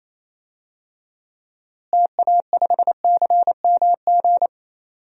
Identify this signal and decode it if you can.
Morse code, heard as a keyed tone, about 28 wpm, 700 Hz. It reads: TA5CMG
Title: DXCC-100 Call Signs - 6 Characters - Repeated 2x - 28wpm